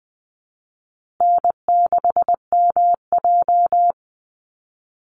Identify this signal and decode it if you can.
Morse code, heard as a keyed tone, about 20 wpm, 700 Hz. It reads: N6MJ
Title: Easy Call Signs Repeated 3x - 20wpm